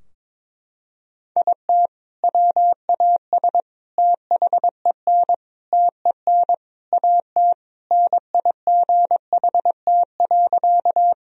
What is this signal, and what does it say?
Morse code, heard as a keyed tone, about 21 wpm, 700 Hz. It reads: IT WAS THEN TEN AT NIGHT.
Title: Sentences from Top 1000 Words - Review Part 2 - 22wpm